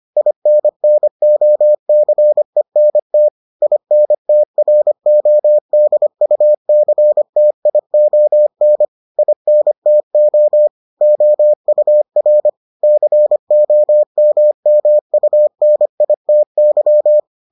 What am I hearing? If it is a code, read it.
Morse code, heard as a keyed tone, about 25 wpm, 600 Hz. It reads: INNOCENT INTRODUCTION INTO OUR COMMUNITY